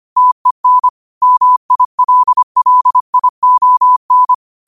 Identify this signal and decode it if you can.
Morse code, heard as a keyed tone, about 25 wpm, 1000 Hz. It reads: TEN MILLION